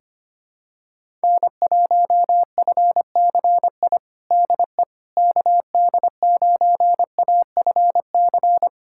Morse code, heard as a keyed tone, about 25 wpm, 700 Hz. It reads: N1FCI DE KD9AFC
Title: USA Callsign DE Callsign 25wpm